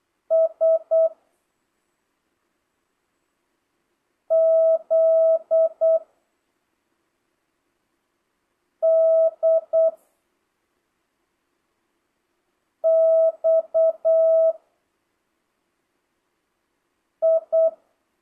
Morse code, heard as a keyed tone, about 8 wpm, 650 Hz. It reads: S Z D X I
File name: jh8cbh / 2025cw